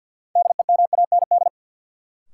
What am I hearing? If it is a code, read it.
Morse code, heard as a keyed tone, about 53 wpm, 700 Hz. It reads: DEMAND
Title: Sentences from Top 800 Words - New Word Drill - 50wpm